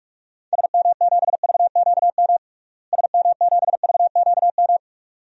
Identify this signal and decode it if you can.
Morse code, heard as a keyed tone, about 46 wpm, 700 Hz. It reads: SM7VQM SM7VQM
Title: DXCC-100 Call Signs - 6 Characters - Repeated 2x - 45wpm